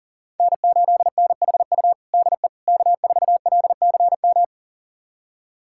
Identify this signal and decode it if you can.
Morse code, heard as a keyed tone, about 40 wpm, 700 Hz. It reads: N8NHU DE K4LCM
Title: USA Callsign DE Callsign - 5 Characters - 40wpm